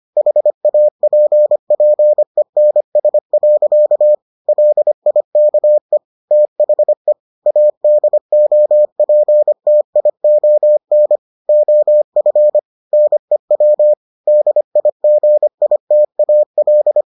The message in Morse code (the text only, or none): HAPPENS. LIKE THE ADOPTION OF NEW DIGITAL